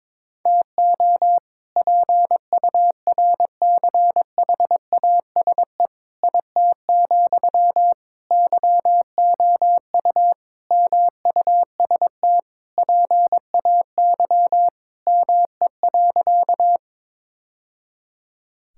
Morse code, heard as a keyed tone, about 21 wpm, 700 Hz. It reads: TO PURCHASE IT, YOU MUST PAY ME.
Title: Sentences from Top 1000 Words - Review Part 5 - 22wpm